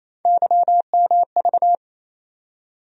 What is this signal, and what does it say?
Morse code, heard as a keyed tone, about 28 wpm, 700 Hz. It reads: YMV
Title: USA Call Sign Postfixes - 3 Letters 28wpm